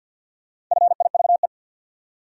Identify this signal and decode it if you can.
Morse code, heard as a keyed tone, about 46 wpm, 700 Hz. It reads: FIVE
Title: Sentences from Top 400 Words - New Word Drill - 50wpm